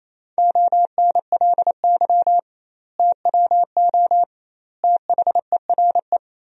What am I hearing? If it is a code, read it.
Morse code, heard as a keyed tone, about 28 wpm, 700 Hz. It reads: ONLY TWO THERE
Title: 2x Word-Spacing - Sentences from Top 200 Words - 28wpm